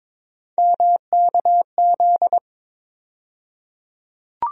Morse code, heard as a keyed tone, about 22 wpm, 700 Hz. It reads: MKZ
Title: USA Call Sign Postfixes - 3 Letters 22wpm